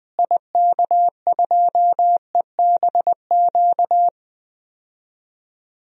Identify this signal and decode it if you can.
Morse code, heard as a keyed tone, about 21 wpm, 700 Hz. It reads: IK2EBQ
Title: DXCC-100 Call Signs - 6 Characters - Repeated 3x - 20wpm